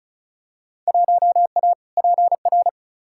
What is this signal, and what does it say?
Morse code, heard as a keyed tone, about 34 wpm, 700 Hz. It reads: 1A PR
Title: Field Day Exchanges - 35wpm